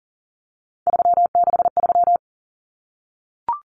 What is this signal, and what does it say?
Morse code, heard as a keyed tone, about 40 wpm, 700 Hz. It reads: 363
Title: Common Cut Numbers - 3-Digit 40wpm